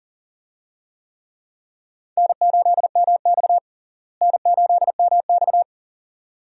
Morse code, heard as a keyed tone, about 40 wpm, 700 Hz. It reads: N8MX N8MX
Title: USA Call Signs - 4 Characters - Repeated 2x - 40wpm